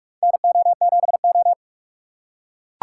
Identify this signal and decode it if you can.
Morse code, heard as a keyed tone, about 41 wpm, 700 Hz. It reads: NO7O
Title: USA Call Signs - 4x Character Spacing - 45wpm